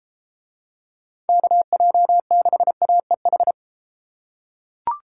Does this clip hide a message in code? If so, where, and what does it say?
Yes: Morse code from 1.3 to 3.5 s, heard as a keyed tone, about 31 wpm, 700 Hz. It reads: KJ6AEH